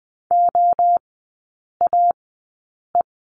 Morse code, heard as a keyed tone, about 20 wpm, spaced as for 8 wpm, 700 Hz. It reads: OAE
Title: 047 Lesson 9e - Number 4 - Three Characters 20wpm